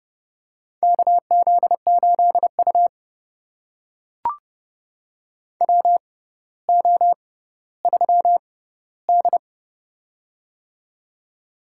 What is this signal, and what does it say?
Morse code, heard as a keyed tone, about 29 wpm, spaced as for 15 wpm, 700 Hz. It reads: KZ8U WO3D
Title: USA Call Signs - 4 Characters - 6x Character Spacing - 30wpm